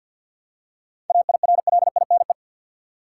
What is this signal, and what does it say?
Morse code, heard as a keyed tone, about 53 wpm, 700 Hz. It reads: AIRLINE
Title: Seven Letter Words - 50wpm